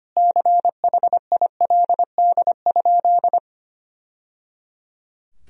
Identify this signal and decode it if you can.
Morse code, heard as a keyed tone, about 25 wpm, 700 Hz. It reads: CHILD?
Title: Sentences from Top 1000 Words - Review Part 4 - 25wpm